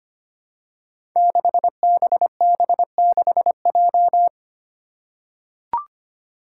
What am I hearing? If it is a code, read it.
Morse code, heard as a keyed tone, about 25 wpm, 700 Hz. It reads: 6BB6J
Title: Five Challenging Letter-Number Combinations 25wpm